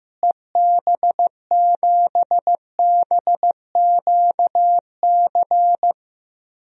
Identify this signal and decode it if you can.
Morse code, heard as a keyed tone, about 15 wpm, 700 Hz. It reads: EB7BQC
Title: DXCC-100 Call Signs - 6 Characters - Repeated 2x - 15wpm